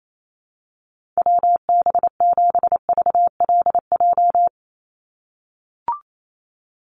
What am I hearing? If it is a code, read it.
Morse code, heard as a keyed tone, about 28 wpm, 700 Hz. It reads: WB7VLJ
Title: USA Call Signs 28wpm